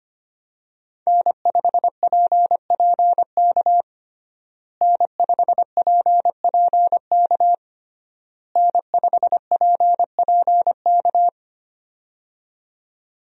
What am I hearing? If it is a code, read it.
Morse code, heard as a keyed tone, about 25 wpm, 700 Hz. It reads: N5PPK N5PPK N5PPK
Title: USA Call Signs - 5 Characters - Repeated 3x - 25wpm